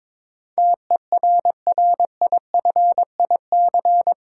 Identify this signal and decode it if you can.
Morse code, heard as a keyed tone, about 22 wpm, 700 Hz. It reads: TERRIFIC
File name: practice / 22wpm / Eight Letter Words - 22wpm